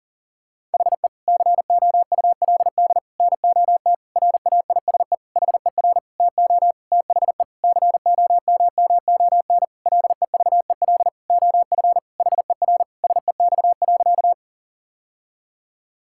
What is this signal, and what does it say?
Morse code, heard as a keyed tone, about 40 wpm, 700 Hz. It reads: HE COULD NOT RAISE HER TO THE COMMON LEVEL OF HER SEX.
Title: Sentences from Top 1600 Words 40wpm